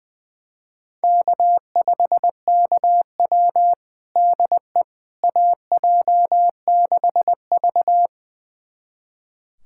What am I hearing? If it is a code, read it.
Morse code, heard as a keyed tone, about 20 wpm, 700 Hz. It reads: K5KW DE AJ6V
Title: USA Callsign DE Callsign - 4 Characters - 20wpm